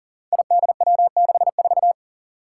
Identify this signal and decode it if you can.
Morse code, heard as a keyed tone, about 40 wpm, 700 Hz. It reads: IDW64
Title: Five Challenging Letter-Number Combinations 40wpm